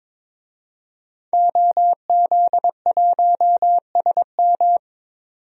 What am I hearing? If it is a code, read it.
Morse code, heard as a keyed tone, about 22 wpm, 700 Hz. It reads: OZ1SM
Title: DXCC-100 Call Signs - 5 Characters - 22wpm